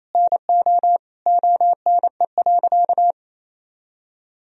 Morse code, heard as a keyed tone, about 28 wpm, 700 Hz. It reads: NO ONE.
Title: Sentences from Top 1000 Words - Review Part 2 - 28wpm